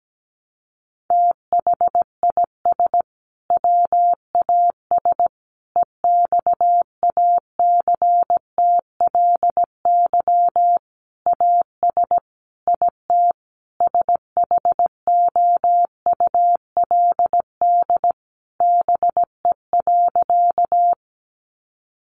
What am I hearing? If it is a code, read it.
Morse code, heard as a keyed tone, about 17 wpm, 700 Hz. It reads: THIS WAS EXACTLY AS IT SHOULD BE.
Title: Sentences from Top 900 Words 17wpm